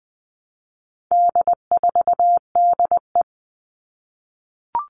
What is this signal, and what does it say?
Morse code, heard as a keyed tone, about 20 wpm, 700 Hz. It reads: D4DE
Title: All Cut Numbers - 4-Digit 20wpm